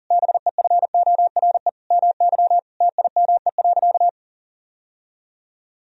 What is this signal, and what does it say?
Morse code, heard as a keyed tone, about 40 wpm, 700 Hz. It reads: BEFORE MY TIME.
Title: Sentences from Top 1000 Words - Review Part 1 - 40wpm